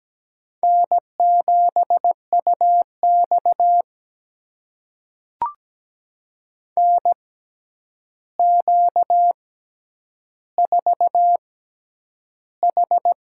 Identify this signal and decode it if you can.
Morse code, heard as a keyed tone, about 17 wpm, spaced as for 9 wpm, 700 Hz. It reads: N7UX NQ4H
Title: USA Call Signs - 4 Characters - 6x Character Spacing - 17wpm